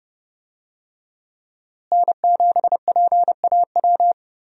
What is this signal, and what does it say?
Morse code, heard as a keyed tone, about 31 wpm, 700 Hz. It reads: N7PAW
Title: USA Call Signs - 5 Characters - Repeated 3x - 30wpm